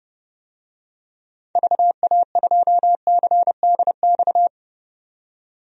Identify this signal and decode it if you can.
Morse code, heard as a keyed tone, about 32 wpm, 700 Hz. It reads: VA2CDX
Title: DXCC-100 Call Signs - 6 Characters - 30wpm